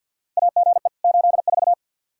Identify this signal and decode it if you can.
Morse code, heard as a keyed tone, about 52 wpm, 700 Hz. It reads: AGE 84
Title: Practice QSO - Exchange 3 - More Info - 50wpm